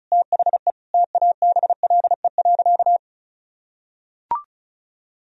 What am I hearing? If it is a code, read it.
Morse code, heard as a keyed tone, about 36 wpm, 700 Hz. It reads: THE TABLE.
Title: Sentences from Top 1000 Words - Review Part 4 - 35wpm